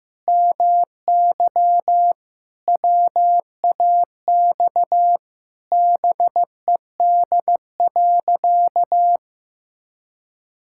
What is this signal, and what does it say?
Morse code, heard as a keyed tone, about 15 wpm, 700 Hz. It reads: MY WAX BED.
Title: Pangram Challenge v2 - 15wpm